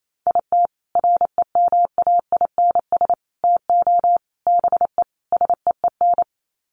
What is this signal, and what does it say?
Morse code, heard as a keyed tone, about 28 wpm, 700 Hz. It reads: IT REMAINS TO BE SEEN
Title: Sets of 5 Words Encore 28wpm